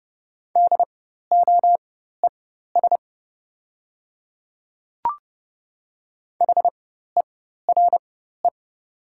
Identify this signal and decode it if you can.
Morse code, heard as a keyed tone, about 30 wpm, spaced as for 14 wpm, 700 Hz. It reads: DOES HERE
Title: 4x Character Spacing - Top 500 Words - 30wpm